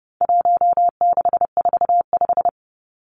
Morse code, heard as a keyed tone, about 30 wpm, 700 Hz. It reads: 1645
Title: Numbers - 4-Digit 30wpm